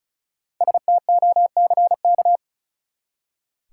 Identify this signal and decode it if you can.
Morse code, heard as a keyed tone, about 36 wpm, 700 Hz. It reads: STOCK
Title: Five Letter Words - 35wpm